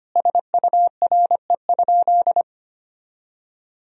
Morse code, heard as a keyed tone, about 25 wpm, 700 Hz. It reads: SURE?
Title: 4x Word-Spacing - Sentences from Top 300 Words - 25wpm